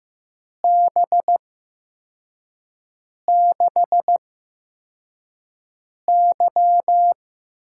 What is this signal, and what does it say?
Morse code, heard as a keyed tone, about 15 wpm, spaced as for 4 wpm, 700 Hz. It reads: B6Y
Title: USA Call Signs - 6 Characters - 8x Character Spacing - 15wpm